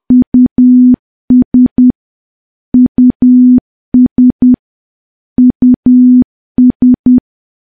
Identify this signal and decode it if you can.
Morse code, heard as a keyed tone, about 10 wpm, 250 Hz. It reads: US US US
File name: morse / morse-training-10WPM